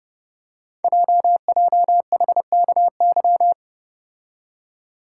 Morse code, heard as a keyed tone, about 29 wpm, 690 Hz. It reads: JJHKY
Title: Code Groups - 5 Letters - 30wpm